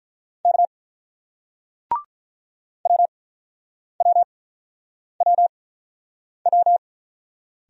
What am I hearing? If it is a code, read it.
Morse code, heard as a keyed tone, about 46 wpm, spaced as for 9 wpm, 700 Hz. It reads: K WWWW